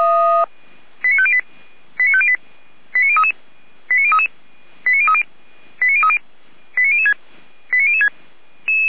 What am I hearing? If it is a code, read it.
Five-tone selective call (ZVEI), heard as quick runs of five tones, tones of about 70 ms. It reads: 89508 89508 89033 89033 89030 89030 89006 89006
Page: /455/SDR/ZVEI/89014/